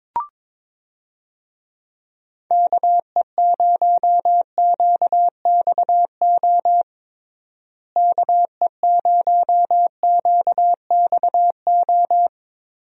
Morse code, heard as a keyed tone, about 22 wpm, 700 Hz. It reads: KE0QXO KE0QXO
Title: USA Call Signs - 6 Characters - Repeated 3x - 22wpm